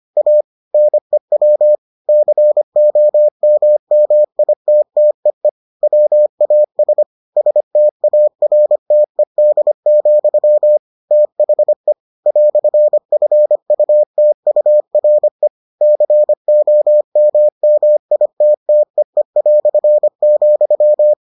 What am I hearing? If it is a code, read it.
Morse code, heard as a keyed tone, about 25 wpm, 600 Hz. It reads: A NEW COMMITTEE WAS STARTED, THE "FUTURE COMMITTEE",